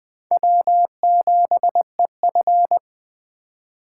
Morse code, heard as a keyed tone, about 21 wpm, 700 Hz. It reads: W7EF
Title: Speed Racing - USA Call Signs 17wpm